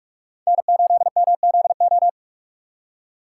Morse code, heard as a keyed tone, about 58 wpm, 700 Hz. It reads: N8MZO